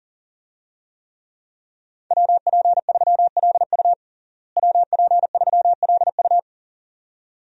Morse code, heard as a keyed tone, about 37 wpm, 700 Hz. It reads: WP3LU WP3LU
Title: USA Call Signs - 5 Characters - Repeated 2x - 40wpm